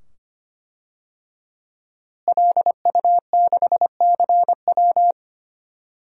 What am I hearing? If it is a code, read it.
Morse code, heard as a keyed tone, about 25 wpm, 700 Hz. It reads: LU6CW
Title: DXCC-100 Call Signs - 5 Characters - Repeated 2x - 25wpm